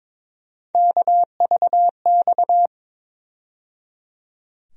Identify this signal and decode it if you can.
Morse code, heard as a keyed tone, about 22 wpm, 700 Hz. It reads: KVX
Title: USA Call Sign Postfixes - 3 Letters 22wpm